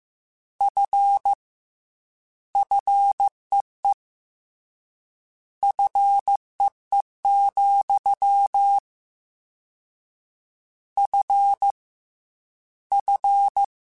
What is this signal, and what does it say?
Morse code, heard as a keyed tone, about 15 wpm, 800 Hz. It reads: F FEE FEE, F F
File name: Combinations / Combinations-mp3s / Combinations@13